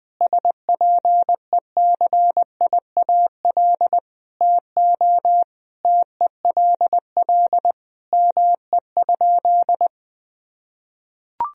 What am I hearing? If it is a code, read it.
Morse code, heard as a keyed tone, about 20 wpm, 700 Hz. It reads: SPECIAL TO TELL ME?
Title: Sentences from Top 1000 Words - Review Part 3 - 20wpm